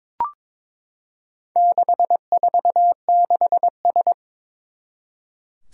Morse code, heard as a keyed tone, about 23 wpm, 700 Hz. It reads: 646S
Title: Four Challenging Letter-Number Combinations 22wpm